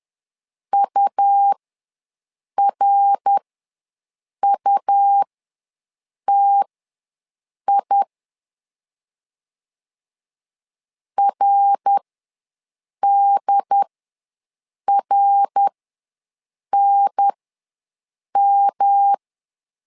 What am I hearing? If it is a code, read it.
Morse code, heard as a keyed tone, about 10 wpm, spaced as for 6 wpm, 800 Hz. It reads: URUTI RDRNM
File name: LEZIONI / 3 MWRO / 3 MWR ws 10wpm 2151